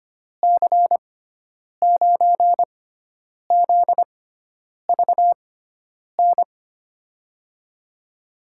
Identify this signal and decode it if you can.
Morse code, heard as a keyed tone, about 25 wpm, spaced as for 9 wpm, 700 Hz. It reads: C9ZVN
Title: USA Call Signs - 6x Character Spacing - 25wpm